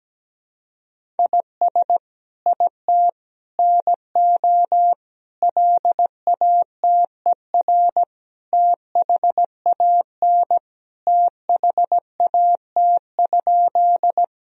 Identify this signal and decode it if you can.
Morse code, heard as a keyed tone, about 17 wpm, 700 Hz. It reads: IS IT NO LATER THAN THAT?
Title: Sentences from Top 1000 Words - Review Part 4 - 17wpm